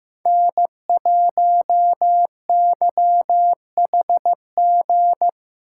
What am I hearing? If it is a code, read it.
Morse code, heard as a keyed tone, about 15 wpm, 700 Hz. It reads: N1YHG
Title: USA Call Signs - 5 Characters - 4x Character Spacing - 15wpm